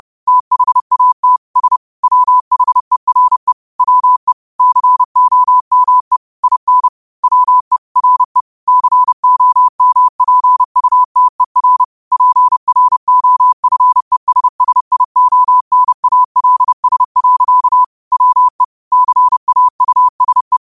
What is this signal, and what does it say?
Morse code, heard as a keyed tone, about 30 wpm, 1000 Hz. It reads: THAT S WHERE WE COME IN WE RE COMPUTER PROFESSIONALS. WE CAUSE